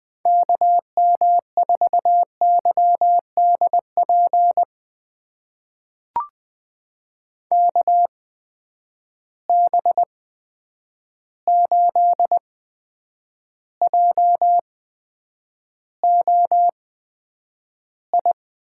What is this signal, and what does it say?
Morse code, heard as a keyed tone, about 20 wpm, spaced as for 9 wpm, 700 Hz. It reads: KM4YDP KB8JOI
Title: USA Call Signs - 6 Characters - 8x Character Spacing - 20wpm